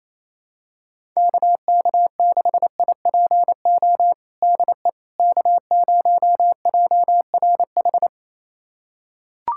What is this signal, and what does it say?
Morse code, heard as a keyed tone, about 28 wpm, 700 Hz. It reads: KK6IPO DE K0JRH
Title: USA Callsign DE Callsign 28wpm